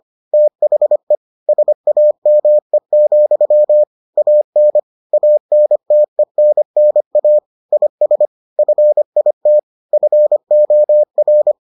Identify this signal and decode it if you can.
Morse code, heard as a keyed tone, about 25 wpm, 600 Hz. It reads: THE SAME, AN ANTENNA IS FIT FOR